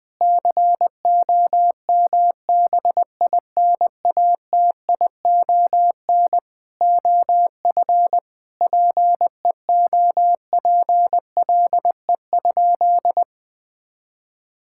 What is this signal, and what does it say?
Morse code, heard as a keyed tone, about 20 wpm, 700 Hz. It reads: COMBINATION OF PEOPLE?
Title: Sentences from Top 1000 Words - Review Part 6 - 20wpm